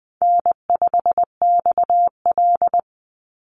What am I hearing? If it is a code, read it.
Morse code, heard as a keyed tone, about 20 wpm, 700 Hz. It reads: N5XL
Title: NA SOTA Chaser Callsigns 20wpm